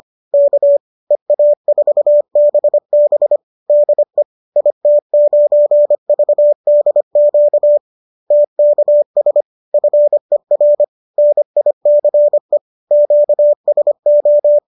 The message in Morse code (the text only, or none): K EA4BB DE IT9VDQ TKS FER NICE QSO